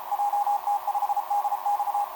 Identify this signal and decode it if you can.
Morse code, heard as a keyed tone, about 34 wpm, 820 Hz, with radio noise. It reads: JT5DX
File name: jt5dx f5idj etc 12m cw 18mar2011 0915